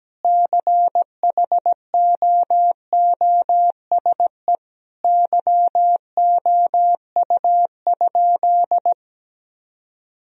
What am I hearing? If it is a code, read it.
Morse code, heard as a keyed tone, about 17 wpm, 700 Hz. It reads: CHOOSE YOU?
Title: Sentences from Top 1000 Words - Review Part 4 - 17wpm